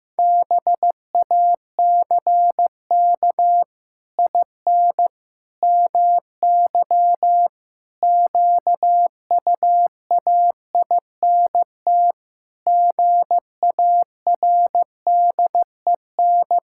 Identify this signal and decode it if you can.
Morse code, heard as a keyed tone, about 15 wpm, 700 Hz. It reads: BACK IN MY QUAINT GARDEN
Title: Pangram Challenge v2 - 15wpm